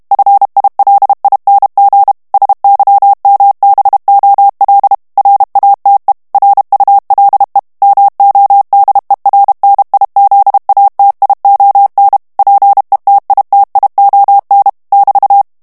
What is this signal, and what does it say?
Morse code, heard as a keyed tone, about 32 wpm, 800 Hz. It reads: FILING SYMBOL RATE RULE MODERNIZATION PETITION <BT>
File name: ANGLAIS texte clair 3 vitesse 30 mots minute